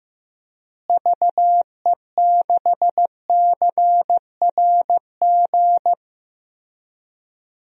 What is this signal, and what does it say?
Morse code, heard as a keyed tone, about 15 wpm, 700 Hz. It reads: VE6CRG
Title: DXCC-100 Call Signs - 6 Characters - Repeated 3x - 15wpm